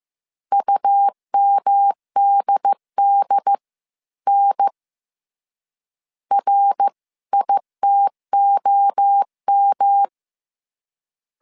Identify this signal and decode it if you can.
Morse code, heard as a keyed tone, about 15 wpm, spaced as for 13 wpm, 800 Hz. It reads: UMDDN RITOM